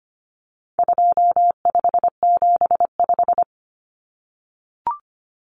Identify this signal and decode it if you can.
Morse code, heard as a keyed tone, about 25 wpm, 700 Hz. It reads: 2575
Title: Numbers - 4-Digit 25wpm